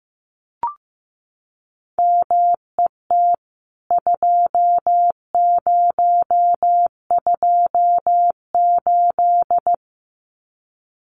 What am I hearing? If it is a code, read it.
Morse code, heard as a keyed tone, about 15 wpm, 700 Hz. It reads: MET 2028
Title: CWT Exchange 15wpm